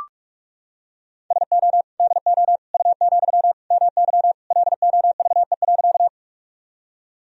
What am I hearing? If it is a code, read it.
Morse code, heard as a keyed tone, about 42 wpm, 700 Hz. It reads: SO DO U, MY LOVE.
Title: Sentences from Top 400 Words and Abbreviations 45wpm